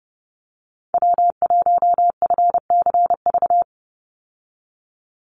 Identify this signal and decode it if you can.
Morse code, heard as a keyed tone, about 30 wpm, 700 Hz. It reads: W1FCV
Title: Call Signs - 30wpm